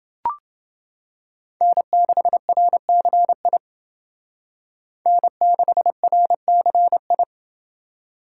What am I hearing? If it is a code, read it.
Morse code, heard as a keyed tone, about 28 wpm, 700 Hz. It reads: N6RCI N6RCI